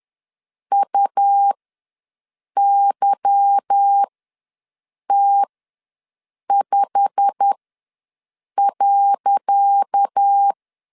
Morse code, heard as a keyed tone, about 10 wpm, spaced as for 6 wpm, 800 Hz. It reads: UYT5.
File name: LEZIONI / 10 SIGNS / 10C DOT ws 10wpm 4213